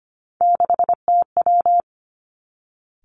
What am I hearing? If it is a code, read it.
Morse code, heard as a keyed tone, about 25 wpm, 700 Hz. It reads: 6TW
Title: Three Challenging Letter-Number Combinations 25wpm